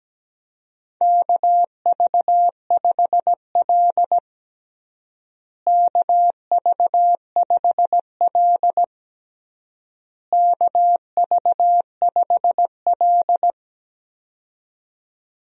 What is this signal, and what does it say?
Morse code, heard as a keyed tone, about 17 wpm, 700 Hz. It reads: KV5L KV5L KV5L
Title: USA Call Signs - 4 Characters - Repeated 3x - 17wpm